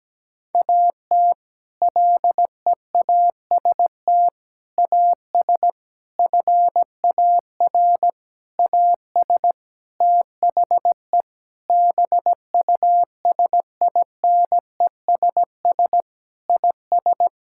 Morse code, heard as a keyed tone, about 17 wpm, 700 Hz. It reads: AT LEAST AS FAR AS THE BUSINESS IS
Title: Sentences from Top 1000 Words - Review Part 6 - 17wpm